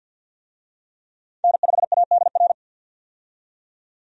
Morse code, heard as a keyed tone, about 47 wpm, 700 Hz. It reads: N5ADR
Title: USA Call Signs - 5 Characters - 50wpm